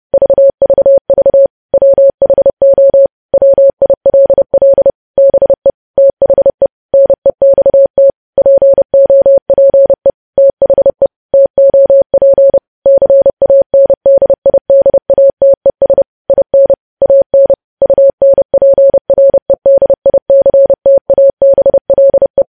Morse code, heard as a keyed tone, about 30 wpm, 550 Hz. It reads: VVV WHO WILL BE THE NEXT POPE THE TOP CANDIDATES IN AN UNPREDICTABLE
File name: podcast / News headlines-1x-30x30